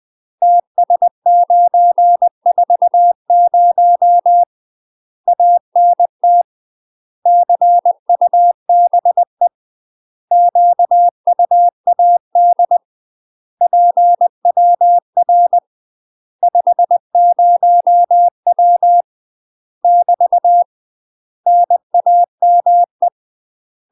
Morse code, heard as a keyed tone, about 20 wpm, 700 Hz. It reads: TS940 ANT CUBE QUAD PWR 50W <BT> NAME
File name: UK MASTER QSOs TRAINING / 20WPM / UK MASTER QSOs 0023